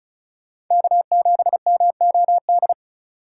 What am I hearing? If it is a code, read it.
Morse code, heard as a keyed tone, about 36 wpm, 700 Hz. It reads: K7MOD
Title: USA Call Signs - 5 Characters - 2x Character Spacing - 35wpm